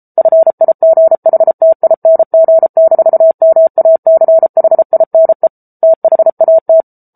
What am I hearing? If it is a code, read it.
Morse code, heard as a keyed tone, about 34 wpm, 650 Hz. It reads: FIGHTING-MACHINE THAT
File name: Book Two / 07-Chapter 7 - 33 WPM